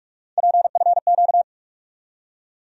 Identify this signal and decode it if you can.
Morse code, heard as a keyed tone, about 48 wpm, 700 Hz. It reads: PFQ
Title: USA Call Sign Postfixes - 3 Letters 45wpm